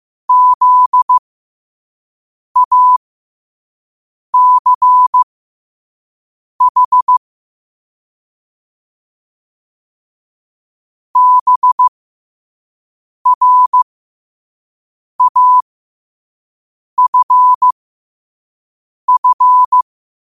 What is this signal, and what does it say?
Morse code, heard as a keyed tone, about 15 wpm, spaced as for 5 wpm, 1000 Hz. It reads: ZACH BRAFF
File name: qotdfiles / 5wpm